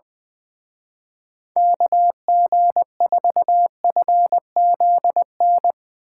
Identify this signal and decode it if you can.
Morse code, heard as a keyed tone, about 21 wpm, 700 Hz. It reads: KG4FZN
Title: Speed Racing - USA 2x3 Call Signs 20wpm